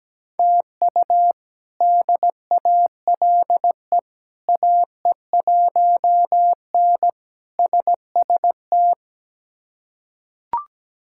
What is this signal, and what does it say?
Morse code, heard as a keyed tone, about 17 wpm, 700 Hz. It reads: TU DALE AE1N SST
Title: K1USN SST QSO Practice - 17wpm